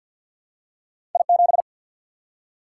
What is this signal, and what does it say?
Morse code, heard as a keyed tone, about 53 wpm, 700 Hz. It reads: I 7
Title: Single-Letter Single-Number 50wpm